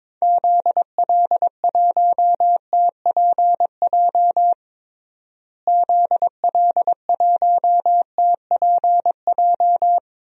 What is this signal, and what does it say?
Morse code, heard as a keyed tone, about 22 wpm, 700 Hz. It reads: ZL1TPJ ZL1TPJ
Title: DXCC-100 Call Signs - 6 Characters - Repeated 3x - 22wpm